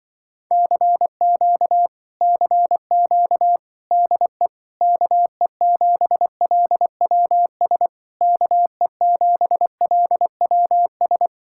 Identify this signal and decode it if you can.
Morse code, heard as a keyed tone, about 24 wpm, 700 Hz. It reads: CQ CQ DE KE7LWS KE7LWS
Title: Speed-Racing Common QSO Phrases - 20wpm